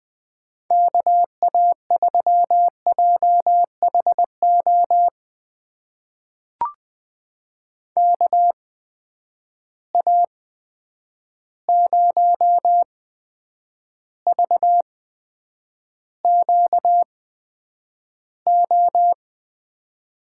Morse code, heard as a keyed tone, about 20 wpm, spaced as for 9 wpm, 700 Hz. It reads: KA3JHO KA0VQO
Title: USA Call Signs - 6 Characters - 8x Character Spacing - 20wpm